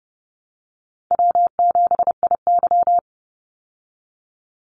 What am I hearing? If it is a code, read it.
Morse code, heard as a keyed tone, about 30 wpm, 700 Hz. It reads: W7IY
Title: Call Signs Repeated 2x - 30wpm